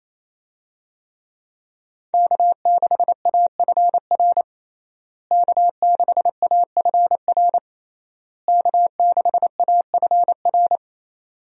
Morse code, heard as a keyed tone, about 28 wpm, 700 Hz. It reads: K6AFR K6AFR K6AFR
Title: USA Call Signs - 5 Characters - Repeated 3x - 28wpm